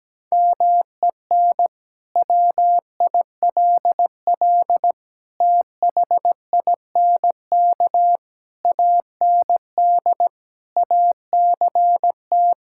Morse code, heard as a keyed tone, about 17 wpm, 700 Hz. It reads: MEN WILL THINK AND ACT
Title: Sentences from Top 1000 Words - Review Part 5 - 17wpm